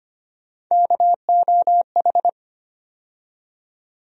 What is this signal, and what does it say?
Morse code, heard as a keyed tone, about 25 wpm, 700 Hz. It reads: KOH
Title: USA Call Sign Postfixes - 3 Letters 25wpm